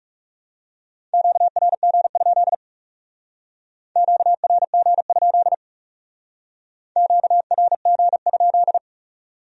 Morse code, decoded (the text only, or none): QRG? QRG? QRG?